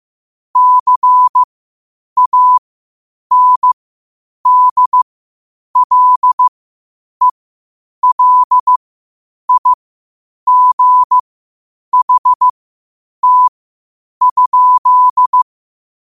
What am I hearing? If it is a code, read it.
Morse code, heard as a keyed tone, about 15 wpm, spaced as for 8 wpm, 1000 Hz. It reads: CANDLELIGHT?
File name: qotdfiles / 8wpm